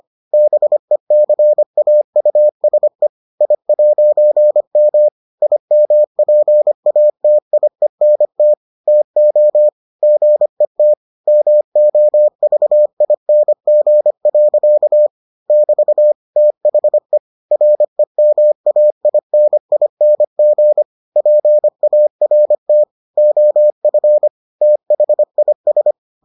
Morse code, heard as a keyed tone, about 25 wpm, 600 Hz. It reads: BECAUSE I'M IMPATIENT TO GET MOVING. <BT> THE REMAINING PART OF THIS